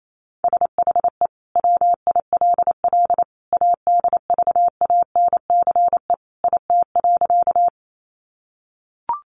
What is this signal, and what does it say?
Morse code, heard as a keyed tone, about 28 wpm, 700 Hz. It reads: SHE WILL ADVANCE IT.
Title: Sentences from Top 1400 Words 28wpm